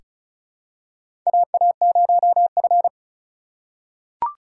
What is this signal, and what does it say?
Morse code, heard as a keyed tone, about 36 wpm, 700 Hz. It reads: AA0F
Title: USA Call Signs - 4 Characters - 3x Character Spacing - 35wpm